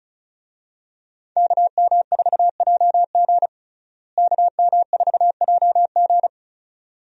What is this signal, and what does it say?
Morse code, heard as a keyed tone, about 35 wpm, 700 Hz. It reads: KM4JG KM4JG
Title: USA Call Signs - 5 Characters - Repeated 2x - 35wpm